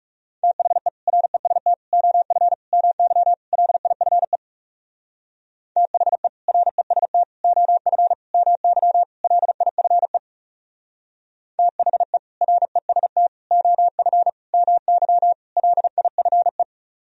Morse code, heard as a keyed tone, about 37 wpm, 700 Hz. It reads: THE REST OF MY LIFE THE REST OF MY LIFE THE REST OF MY LIFE
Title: Speed-Racing - Sets of 5 Words 30wpm